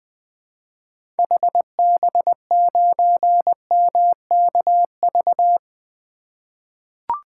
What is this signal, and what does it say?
Morse code, heard as a keyed tone, about 20 wpm, 700 Hz. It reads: HB9MKV
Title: 2023 SOTA Chaser Call Signs - 20wpm